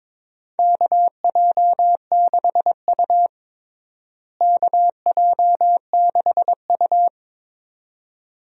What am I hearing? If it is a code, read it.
Morse code, heard as a keyed tone, about 22 wpm, 700 Hz. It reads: KJ6U KJ6U
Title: USA Call Signs - 4 Characters - Repeated 2x - 22wpm